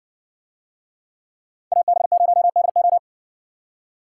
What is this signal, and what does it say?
Morse code, heard as a keyed tone, about 60 wpm, 700 Hz. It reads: AB0DO